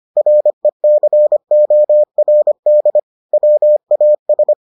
Morse code, heard as a keyed tone, about 25 wpm, 600 Hz. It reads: RECORD WAS